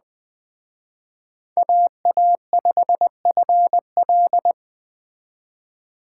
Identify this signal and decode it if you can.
Morse code, heard as a keyed tone, about 20 wpm, 700 Hz. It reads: AA5FL
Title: Speed Racing - USA 2x2 Call Signs 17wpm